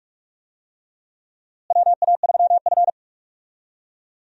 Morse code, heard as a keyed tone, about 41 wpm, 700 Hz. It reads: WA3F
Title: USA Call Signs - 4 Characters - Repeated 3x - 45wpm